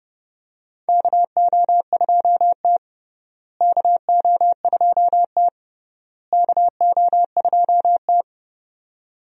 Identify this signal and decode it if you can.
Morse code, heard as a keyed tone, about 30 wpm, 700 Hz. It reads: KO2T KO2T KO2T
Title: USA Call Signs - 4 Characters - Repeated 3x - 30wpm